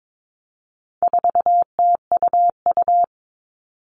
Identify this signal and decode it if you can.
Morse code, heard as a keyed tone, about 22 wpm, 700 Hz. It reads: 4TUU
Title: All Cut Numbers - 4-Digit 22wpm